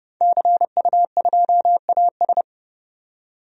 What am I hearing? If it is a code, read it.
Morse code, heard as a keyed tone, about 30 wpm, 700 Hz. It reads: CU2AS
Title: DXCC-100 Call Signs - 5 Characters - 30wpm